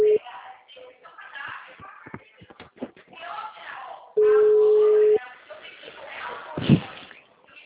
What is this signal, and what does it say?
Ringback tone (Europe, 425 Hz), with speech.
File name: out- -2010- - - .13826